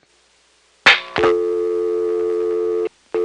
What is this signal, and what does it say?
Dial tone (North America), 350 + 440 Hz.